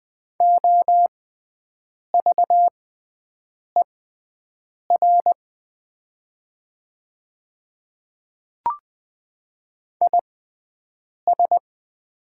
Morse code, heard as a keyed tone, about 20 wpm, spaced as for 7 wpm, 700 Hz. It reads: OVER IS
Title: 6x Character Spacing - Top 200 Words - 20wpm